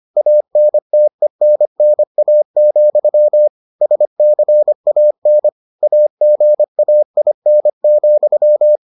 Morse code, heard as a keyed tone, about 25 wpm, 600 Hz. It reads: ANTENNA, SCAN AGAIN,